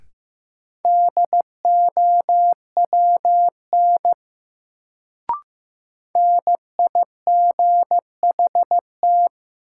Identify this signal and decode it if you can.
Morse code, heard as a keyed tone, about 15 wpm, 700 Hz. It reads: DOWN NIGHT
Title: Sentences from Top 300 Words - New Word Drill - 15wpm